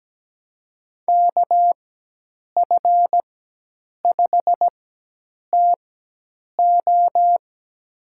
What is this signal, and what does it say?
Morse code, heard as a keyed tone, about 17 wpm, spaced as for 8 wpm, 700 Hz. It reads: KF5TO
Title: USA Call Signs - 5 Characters - 4x Character Spacing - 17wpm